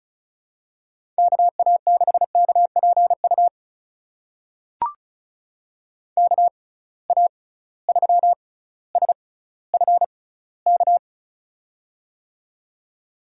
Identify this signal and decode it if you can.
Morse code, heard as a keyed tone, about 36 wpm, spaced as for 18 wpm, 700 Hz. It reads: KA6KPU KA3SFK
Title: USA Call Signs - 6 Characters - 6x Character Spacing - 35wpm